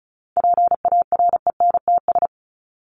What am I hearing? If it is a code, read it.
Morse code, heard as a keyed tone, about 34 wpm, 700 Hz. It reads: PARENTS
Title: ES:S Suffix - Top 100 Words with Base Words 35wpm